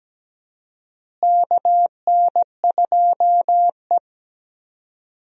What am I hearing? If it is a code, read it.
Morse code, heard as a keyed tone, about 17 wpm, 700 Hz. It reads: KN2E
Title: USA Call Signs - 4 Characters - Repeated 3x - 17wpm